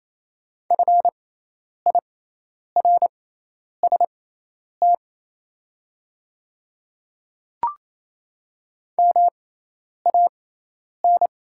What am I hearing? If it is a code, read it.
Morse code, heard as a keyed tone, about 28 wpm, spaced as for 10 wpm, 700 Hz. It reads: FIRST MAN
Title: 6x Character Spacing - Top 300 Words - 28wpm